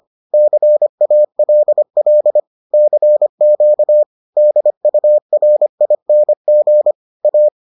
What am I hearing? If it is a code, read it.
Morse code, heard as a keyed tone, about 25 wpm, 600 Hz. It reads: CALL CQ DURING A